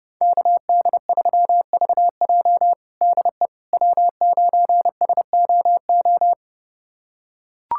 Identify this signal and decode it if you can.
Morse code, heard as a keyed tone, about 32 wpm, 700 Hz. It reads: KD3VJ DE W9SOO
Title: USA Callsign DE Callsign - 5 Characters - 30wpm